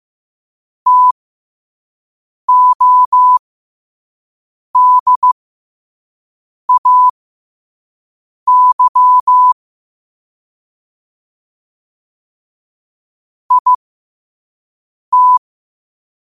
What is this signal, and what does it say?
Morse code, heard as a keyed tone, about 15 wpm, spaced as for 5 wpm, 1000 Hz. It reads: TODAY IT